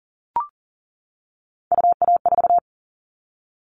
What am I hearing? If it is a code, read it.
Morse code, heard as a keyed tone, about 39 wpm, 700 Hz. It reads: UA4
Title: Callsign Prefixes for Top 50 DXCC - 40wpm